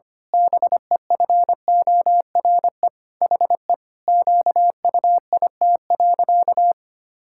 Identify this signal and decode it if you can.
Morse code, heard as a keyed tone, about 25 wpm, 700 Hz. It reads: BEFORE HE QUIT.
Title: Pangram Challenge v2 - 25wpm